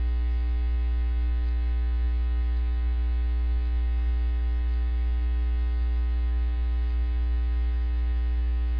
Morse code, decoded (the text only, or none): none